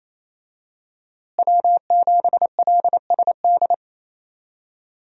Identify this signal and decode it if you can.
Morse code, heard as a keyed tone, about 28 wpm, 700 Hz. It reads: W7LSD
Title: USA Call Signs - 5 Characters - Repeated 3x - 28wpm